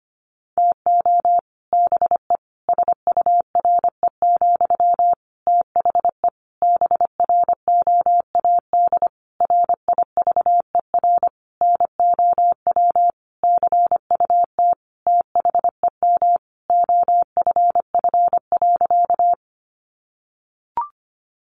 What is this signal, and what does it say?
Morse code, heard as a keyed tone, about 25 wpm, 700 Hz. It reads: TO BE SURE, THE BROAD RIVER NOW CUT THEM OFF.
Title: Sentences from Top 1000 Words 25wpm